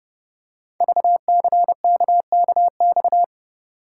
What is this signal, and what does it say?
Morse code, heard as a keyed tone, about 30 wpm, 700 Hz. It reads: VCKKX
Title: Code Groups - 5 Letters - 30wpm